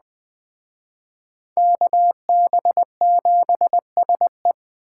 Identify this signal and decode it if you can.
Morse code, heard as a keyed tone, about 20 wpm, 700 Hz. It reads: KB7SE
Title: Speed Racing - USA 2x2 Call Signs 17wpm